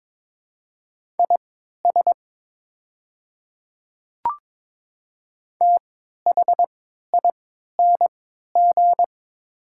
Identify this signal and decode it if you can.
Morse code, heard as a keyed tone, about 22 wpm, spaced as for 13 wpm, 700 Hz. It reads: IS THING
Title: 3x Character Spacing - Top 100 Words - 22wpm